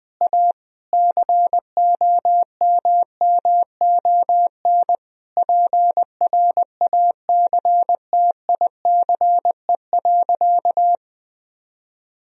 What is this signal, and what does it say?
Morse code, heard as a keyed tone, about 20 wpm, 700 Hz. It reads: A COMMON PRACTICE.
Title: Sentences from Top 1000 Words - Review Part 4 - 20wpm